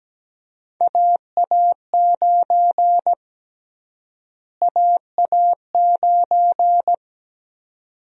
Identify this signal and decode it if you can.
Morse code, heard as a keyed tone, about 17 wpm, 700 Hz. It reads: AA9 AA9
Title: USA Call Sign Prefixes - Repeated 2x - 17wpm